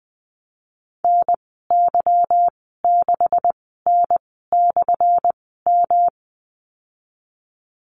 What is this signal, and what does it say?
Morse code, heard as a keyed tone, about 19 wpm, spaced as for 14 wpm, 700 Hz. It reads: NY6N/M
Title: 179 Lesson 29g - Slash - Call Signs 20wpm